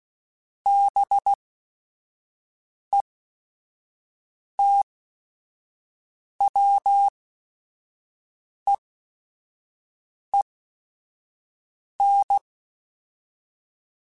Morse code, heard as a keyed tone, about 16 wpm, spaced as for 5 wpm, 800 Hz. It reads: BETWEEN
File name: Weather Sets / Weather Sets-mp3s / Weather 5 22 @5